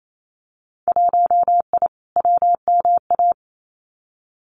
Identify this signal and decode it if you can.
Morse code, heard as a keyed tone, about 28 wpm, 700 Hz. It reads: 1I WMA
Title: Winter Field Day Exchanges Encore 28wpm